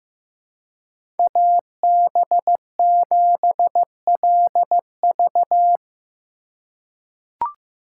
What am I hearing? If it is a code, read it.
Morse code, heard as a keyed tone, about 15 wpm, 700 Hz. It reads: AB7LV
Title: USA Call Signs - 5 Characters - 3x Character Spacing - 15wpm